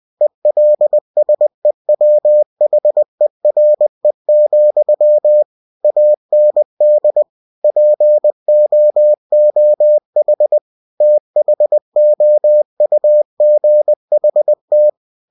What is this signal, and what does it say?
Morse code, heard as a keyed tone, about 20 wpm, 600 Hz. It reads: ELSEWHERE, AND POOH THOUGHT
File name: cw-books / winnie-the-pooh / winnie-20wpm-page-0023